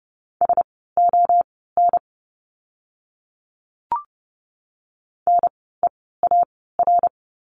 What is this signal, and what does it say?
Morse code, heard as a keyed tone, about 30 wpm, spaced as for 17 wpm, 700 Hz. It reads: SON NEAR
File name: Lesson 16f - Letter c - Words 30wpm